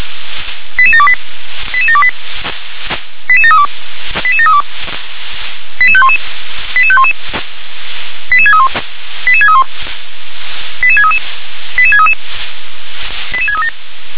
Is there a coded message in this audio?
Yes: five-tone selective call (ZVEI) from 0.8 to 13.7 s, heard as quick runs of five tones, tones of about 70 ms. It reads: 88728 88728 88732 88732 88511 88511 88631 88631 88633 88633 88637